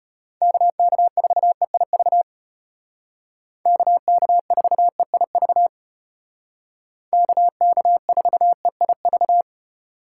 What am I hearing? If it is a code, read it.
Morse code, heard as a keyed tone, about 33 wpm, 700 Hz. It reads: KK4EIV KK4EIV KK4EIV